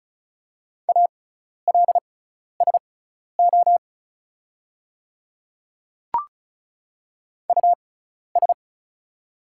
Morse code, heard as a keyed tone, about 34 wpm, spaced as for 12 wpm, 700 Hz. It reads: ALSO US
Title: 6x Character Spacing - Top 300 Words - 35wpm